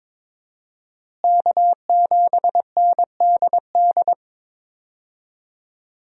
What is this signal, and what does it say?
Morse code, heard as a keyed tone, about 21 wpm, 700 Hz. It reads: K7NDD
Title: USA Call Signs - 5 Characters - Repeated 3x - 22wpm